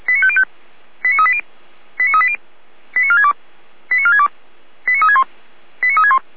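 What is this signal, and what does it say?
Five-tone selective call (ZVEI), heard as quick runs of five tones, tones of about 70 ms. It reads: 89380 89380 89462 89462 89361 89361
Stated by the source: Five-tone ZVEI selective call received by SDR.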